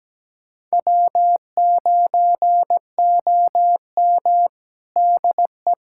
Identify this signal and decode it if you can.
Morse code, heard as a keyed tone, about 17 wpm, 700 Hz. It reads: W9OM DE
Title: USA Callsign DE Callsign - 4 Characters - 17wpm